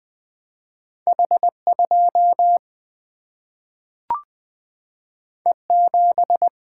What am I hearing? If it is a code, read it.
Morse code, heard as a keyed tone, about 20 wpm, 700 Hz. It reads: H 2 E 7
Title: Single-Letter Single-Number 20wpm